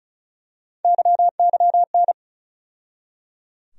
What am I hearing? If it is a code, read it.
Morse code, heard as a keyed tone, about 34 wpm, 700 Hz. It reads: YYN
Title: USA Call Sign Postfixes - 3 Letters 35wpm